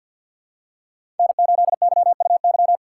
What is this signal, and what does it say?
Morse code, heard as a keyed tone, about 53 wpm, 700 Hz. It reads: N8YUY
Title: USA Call Signs - 5 Characters - 4x Character Spacing - 50wpm